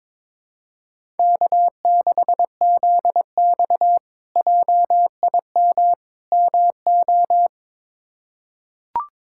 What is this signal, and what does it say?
Morse code, heard as a keyed tone, about 21 wpm, 700 Hz. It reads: K6ZX JIM MO
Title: K1USN SST QSO Practice - 22wpm